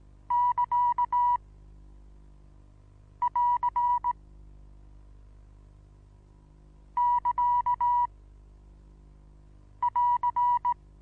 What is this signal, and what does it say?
Morse code, heard as a keyed tone, about 18 wpm, spaced as for 4 wpm, 970 Hz. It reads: <KA><AR><KA><AR>
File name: Onlinelehrgang / morsen / DJ4UF Morsen25